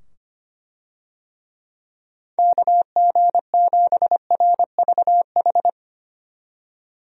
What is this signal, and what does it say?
Morse code, heard as a keyed tone, about 25 wpm, 700 Hz. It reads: KG7RVH
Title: USA Call Signs - 6 Characters - 25wpm